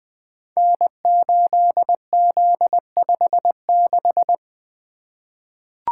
Morse code, heard as a keyed tone, about 20 wpm, 700 Hz. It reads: N8Z56
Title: Code Groups - 5 Letter:Numbers - 20wpm